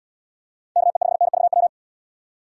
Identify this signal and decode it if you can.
Morse code, heard as a keyed tone, about 71 wpm, 700 Hz. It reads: KE5NVW